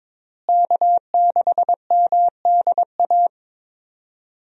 Morse code, heard as a keyed tone, about 22 wpm, 700 Hz. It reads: K6MDA
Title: USA Call Signs - 5 Characters - Repeated 3x - 22wpm